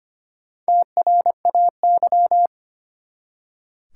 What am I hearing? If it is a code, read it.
Morse code, heard as a keyed tone, about 25 wpm, 700 Hz. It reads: TRAY
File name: practice / 25wpm / Easy Words with Hints - 25wpm